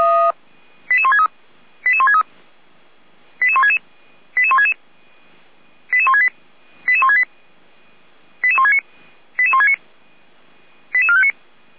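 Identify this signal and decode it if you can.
Five-tone selective call (ZVEI), heard as quick runs of five tones, tones of about 70 ms. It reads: 88163 88163 88166 88166 88168 88168 88169 88169 88459
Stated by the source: ZVEI five-tone selective call received by SDR.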